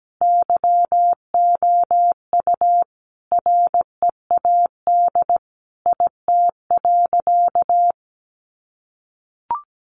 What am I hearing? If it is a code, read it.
Morse code, heard as a keyed tone, about 17 wpm, 700 Hz. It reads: YOU READ IT.
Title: Sentences from Top 1300 Words 17wpm